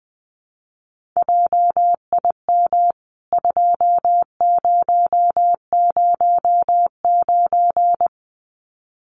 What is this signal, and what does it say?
Morse code, heard as a keyed tone, about 20 wpm, 700 Hz. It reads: JIM 2009
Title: CWT Exchange 20wpm